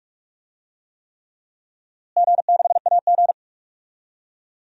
Morse code, heard as a keyed tone, about 46 wpm, 700 Hz. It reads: G6AG
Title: DXCC-100 Call Signs - 4 Characters - Repeated 3x - 45wpm